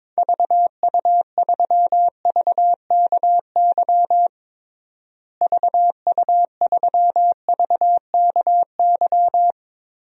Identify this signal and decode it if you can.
Morse code, heard as a keyed tone, about 22 wpm, 700 Hz. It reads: VU3VKY VU3VKY
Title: DXCC-100 Call Signs - 6 Characters - Repeated 3x - 22wpm